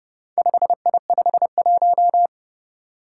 Morse code, heard as a keyed tone, about 30 wpm, 700 Hz. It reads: 5I51
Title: Four Challenging Letter-Number Combinations 30wpm